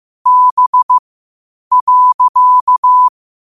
Morse code, heard as a keyed tone, about 15 wpm, spaced as for 8 wpm, 1000 Hz. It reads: B.
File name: qotdfiles / 8wpm